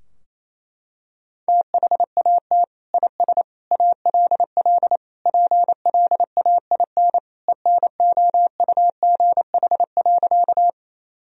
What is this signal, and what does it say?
Morse code, heard as a keyed tone, about 28 wpm, 700 Hz. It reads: THAT IS ALL PLAIN ENOUGH.
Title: Sentences from Top 1000 Words - Review Part 2 - 28wpm